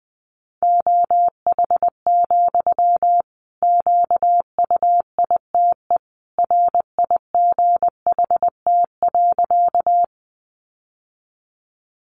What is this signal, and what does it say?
Morse code, heard as a keyed tone, about 20 wpm, 700 Hz. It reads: OH, QUITE RIGHT.
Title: Sentences from Top 700 Words 20wpm